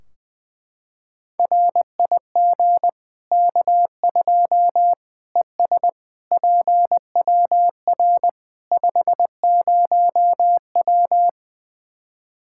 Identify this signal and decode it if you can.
Morse code, heard as a keyed tone, about 20 wpm, 700 Hz. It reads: RIG K2 ES PWR 50W
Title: Practice QSO - Exchange 2 - Info - 20wpm